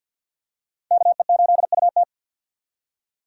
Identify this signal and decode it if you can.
Morse code, heard as a keyed tone, about 52 wpm, 700 Hz. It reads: KE8UT
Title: USA Call Signs - 5 Characters - 2x Character Spacing - 50wpm